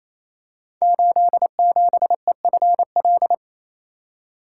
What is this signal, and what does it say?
Morse code, heard as a keyed tone, about 27 wpm, 700 Hz. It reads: 87EFL
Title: Code Groups - 5 Letter:Numbers - 28wpm